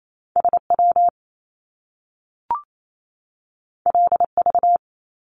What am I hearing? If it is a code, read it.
Morse code, heard as a keyed tone, about 28 wpm, 700 Hz. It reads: SW LV
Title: Top Two Letter Combinations in Words 28wpm